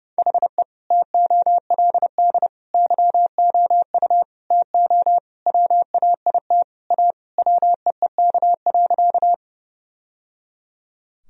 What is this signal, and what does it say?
Morse code, heard as a keyed tone, about 32 wpm, 700 Hz. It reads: HE TOLD YOU TO WAIT A WEEK.
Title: Sentences from Top 1000 Words - Review Part 2 - 30wpm